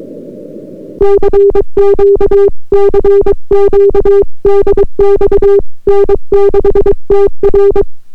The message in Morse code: CQ CQ DX N6TR